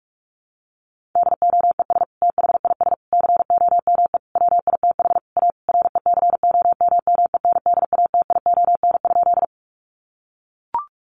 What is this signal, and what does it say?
Morse code, heard as a keyed tone, about 44 wpm, 700 Hz. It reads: DOES THIS COME WITH A RECOMMENDATION?
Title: Sentences from Top 1400 Words 45wpm